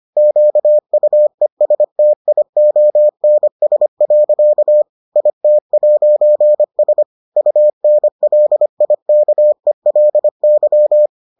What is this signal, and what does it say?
Morse code, heard as a keyed tone, about 25 wpm, 600 Hz. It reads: QUESTIONS. IT'S UNLIKELY